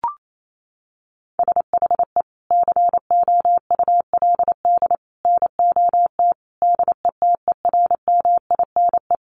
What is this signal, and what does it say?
Morse code, heard as a keyed tone, about 28 wpm, 700 Hz. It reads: SHE COULD NOT DETERMINE
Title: Sentences from Top 700 Words 28wpm